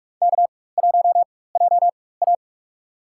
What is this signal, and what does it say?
Morse code, heard as a keyed tone, about 41 wpm, spaced as for 20 wpm, 700 Hz. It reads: K1JA
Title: USA Call Signs - 4 Characters - 4x Character Spacing - 45wpm